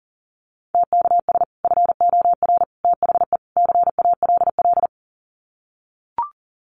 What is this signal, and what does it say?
Morse code, heard as a keyed tone, about 40 wpm, 700 Hz. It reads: TKS FOR THE CALL
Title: Common QSO Phrases 40wpm